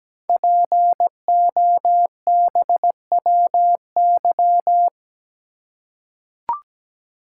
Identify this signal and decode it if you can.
Morse code, heard as a keyed tone, about 17 wpm, 700 Hz. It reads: POBWY
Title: Code Groups - 5 Letters - 17wpm